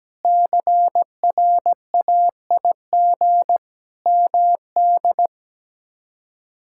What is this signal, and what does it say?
Morse code, heard as a keyed tone, about 17 wpm, 700 Hz. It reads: CRAIG MD
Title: K1USN SST QSO Practice - 17wpm